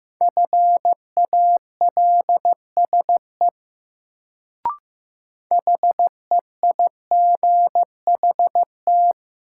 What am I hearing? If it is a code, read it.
Morse code, heard as a keyed tone, about 15 wpm, 700 Hz. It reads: FALSE HEIGHT
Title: Sentences from Top 1600 Words - New Word Drill - 15wpm